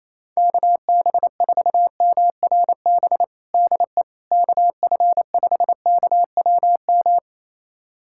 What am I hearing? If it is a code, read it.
Morse code, heard as a keyed tone, about 28 wpm, 700 Hz. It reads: KB4MRB DE KF5KWM
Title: USA Callsign DE Callsign 28wpm